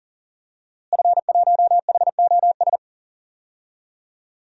Morse code, heard as a keyed tone, about 40 wpm, 700 Hz. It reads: F1HOS
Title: DXCC-100 Call Signs - 5 Characters - Repeated 2x - 40wpm